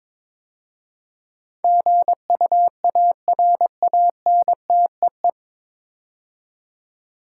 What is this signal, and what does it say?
Morse code, heard as a keyed tone, about 22 wpm, 700 Hz. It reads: GUARANTEE
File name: 9 Letter Words - First 3 Letters Given - 22wpm